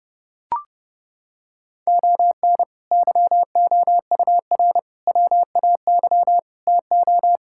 Morse code, heard as a keyed tone, about 30 wpm, 700 Hz. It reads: ON YOUR WAY TO
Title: Sentences from Top 1000 Words - Review Part 6 - 30wpm